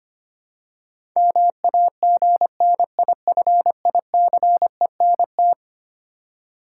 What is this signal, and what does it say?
Morse code, heard as a keyed tone, about 25 wpm, 700 Hz. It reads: MAGNIFICENT
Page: morse-code-ninja-practice-25wpm